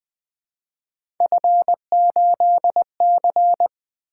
Morse code, heard as a keyed tone, about 20 wpm, 700 Hz. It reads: F8C
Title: Three Letter-Number Combinations - 20wpm